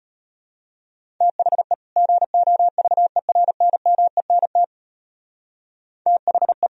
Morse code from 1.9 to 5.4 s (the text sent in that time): GOVERNMENT